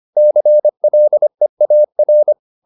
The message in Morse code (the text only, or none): CLEAR